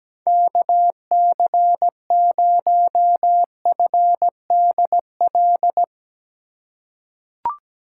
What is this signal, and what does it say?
Morse code, heard as a keyed tone, about 17 wpm, 700 Hz. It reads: KC0FDL
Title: USA Call Signs - 4x Character Spacing - 17wpm